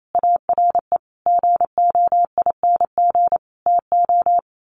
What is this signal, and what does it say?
Morse code, heard as a keyed tone, about 28 wpm, 700 Hz. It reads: ARE GOING TO
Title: Sets of 5 Words Encore2 28wpm